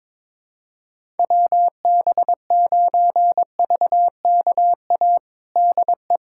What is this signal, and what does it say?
Morse code, heard as a keyed tone, about 22 wpm, 700 Hz. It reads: WB9VKA DE
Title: USA Callsign DE Callsign 22wpm